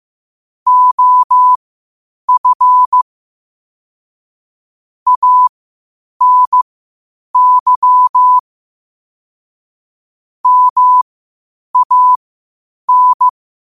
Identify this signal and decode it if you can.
Morse code, heard as a keyed tone, about 15 wpm, spaced as for 8 wpm, 1000 Hz. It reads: OF ANY MAN
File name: qotdfiles / 8wpm